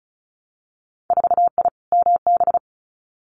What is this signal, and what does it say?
Morse code, heard as a keyed tone, about 36 wpm, 700 Hz. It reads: 4I MB
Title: Winter Field Day Exchanges Encore 35wpm